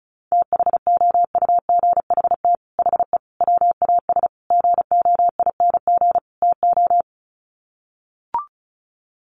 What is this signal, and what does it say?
Morse code, heard as a keyed tone, about 34 wpm, 700 Hz. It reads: THOUGHT HE WAS GOING TO
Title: Sets of 5 Words Encore2 35wpm